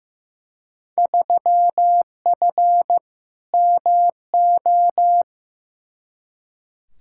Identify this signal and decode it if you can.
Morse code, heard as a keyed tone, about 15 wpm, 700 Hz. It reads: 3F MO
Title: Field Day Exchanges Encore - 15wpm